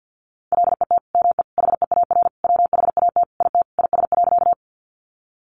Morse code, heard as a keyed tone, about 50 wpm, 700 Hz. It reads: LET ME HEAR WHAT IT IS.
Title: Sentences from Top 300 Words 50wpm